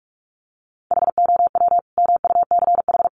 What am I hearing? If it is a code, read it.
Morse code, heard as a keyed tone, about 44 wpm, 700 Hz. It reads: HOW MUCH
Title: Sentences from Top 1200 Words 45wpm